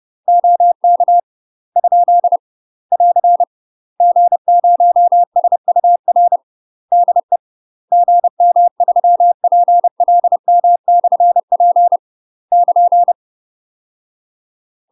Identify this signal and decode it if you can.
Morse code, heard as a keyed tone, about 30 wpm, 700 Hz. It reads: OK ? <AR> G0SUR DE GM3PLM/P <KN>
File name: UK MASTER QSOs TRAINING / 30WPM / UK MASTER QSOs 0008